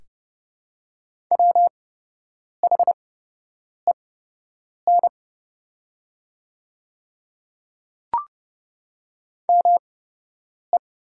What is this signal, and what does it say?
Morse code, heard as a keyed tone, about 30 wpm, spaced as for 8 wpm, 700 Hz. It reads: WHEN ME
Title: 8x Character Spacing - Top 200 Words - 30wpm